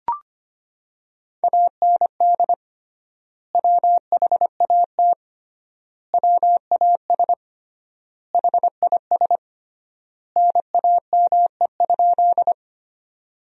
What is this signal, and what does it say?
Morse code, heard as a keyed tone, about 25 wpm, 700 Hz. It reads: AND WHAT WAS HIS NAME?
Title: 3x Word-Spacing - Sentences from Top 200 Words - 25wpm